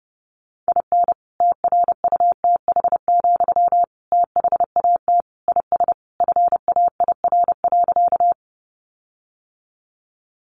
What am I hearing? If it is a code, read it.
Morse code, heard as a keyed tone, about 30 wpm, 700 Hz. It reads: IN TRUTH, THAT IS FAIR.
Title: Sentences from Top 1300 Words 30wpm